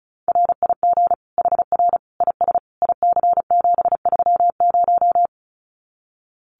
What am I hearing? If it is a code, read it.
Morse code, heard as a keyed tone, about 34 wpm, 700 Hz. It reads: RIG HR IS IC730
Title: Common QSO Phrases 35wpm